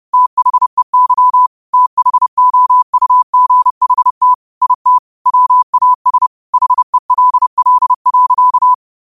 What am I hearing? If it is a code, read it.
Morse code, heard as a keyed tone, about 30 wpm, 1000 Hz. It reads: THEY THOUGHT IT WAS HELL.